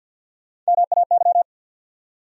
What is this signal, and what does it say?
Morse code, heard as a keyed tone, about 67 wpm, spaced as for 58 wpm, 700 Hz. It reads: MAY
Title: Speed-Racing Top 100 Words 50wpm